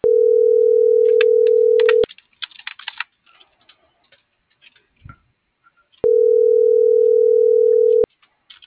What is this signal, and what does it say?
Ringback tone (North America), 440 + 480 Hz.
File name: CENTRAL MABE / out sebastian.espinosa